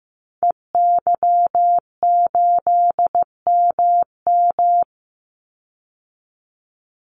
Morse code, heard as a keyed tone, about 15 wpm, 700 Hz. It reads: EY8MM
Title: Call Signs Repeated 2x - 15wpm